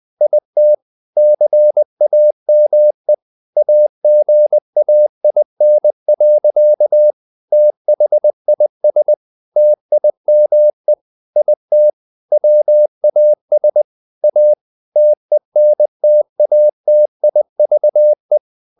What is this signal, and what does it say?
Morse code, heard as a keyed tone, about 20 wpm, 600 Hz. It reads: IT CAME AGAIN. THIS TIME IT WAS A TENTATIVE
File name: the old man and the sea / the old man and the sea part 1 20wpm